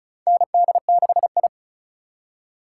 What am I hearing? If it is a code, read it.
Morse code, heard as a keyed tone, about 36 wpm, 700 Hz. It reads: ND6I
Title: USA Call Signs - 4 Characters - 3x Character Spacing - 35wpm